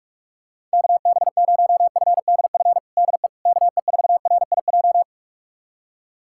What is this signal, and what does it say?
Morse code, heard as a keyed tone, about 42 wpm, 690 Hz. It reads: KB0FDF DE KE4RIJ
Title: USA Callsign DE Callsign 45wpm